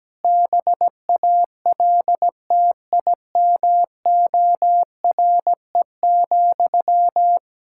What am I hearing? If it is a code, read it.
Morse code, heard as a keyed tone, about 17 wpm, 700 Hz. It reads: BALTIMORE,
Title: Practice QSO - Exchange 1 - Beginning - 17wpm